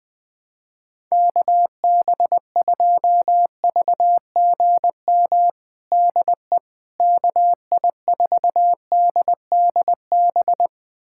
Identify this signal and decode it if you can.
Morse code, heard as a keyed tone, about 20 wpm, 700 Hz. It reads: KB2VGM DE KI4DDB
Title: USA Callsign DE Callsign - 6 Characters - 20wpm